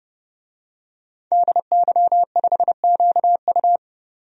Morse code, heard as a keyed tone, about 30 wpm, 700 Hz. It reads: DY5QU
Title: DXCC-100 Call Signs - 5 Characters - 30wpm